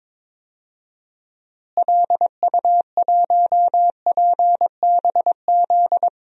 Morse code, heard as a keyed tone, about 22 wpm, 700 Hz. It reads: LU1PBZ
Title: DXCC-100 Call Signs - 6 Characters - Repeated 2x - 22wpm